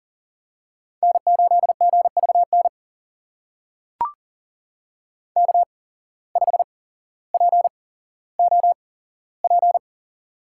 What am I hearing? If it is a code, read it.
Morse code, heard as a keyed tone, about 39 wpm, spaced as for 17 wpm, 700 Hz. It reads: N8GVN K5POP
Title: USA Call Signs - 5 Characters - 8x Character Spacing - 40wpm